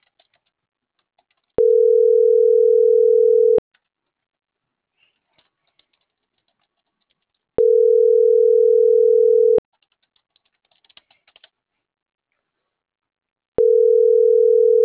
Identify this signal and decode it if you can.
Ringback tone (North America), 440 + 480 Hz.